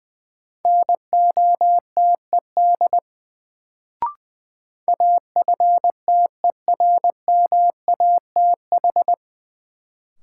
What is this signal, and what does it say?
Morse code, heard as a keyed tone, about 20 wpm, 700 Hz. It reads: NOTED AFTERMATH
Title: Sets of 5 Words - New Word Drill - 20wpm